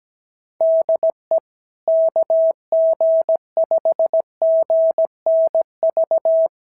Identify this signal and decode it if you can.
Morse code, heard as a keyed tone, about 17 wpm, 650 Hz. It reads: DE KG5GNV
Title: SOTA QSO Practice 17wpm